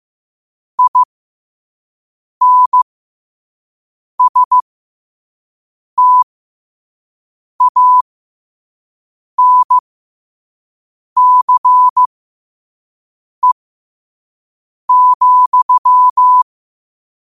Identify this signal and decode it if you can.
Morse code, heard as a keyed tone, about 15 wpm, spaced as for 5 wpm, 1000 Hz. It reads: INSTANCE,
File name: qotdfiles / 5wpm